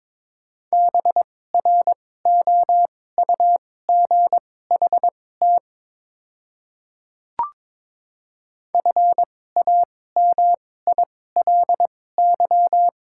Morse code, heard as a keyed tone, about 22 wpm, spaced as for 16 wpm, 690 Hz. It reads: BROUGHT FAMILY
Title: 2x Character Spacing - Top 500 Words - 22wpm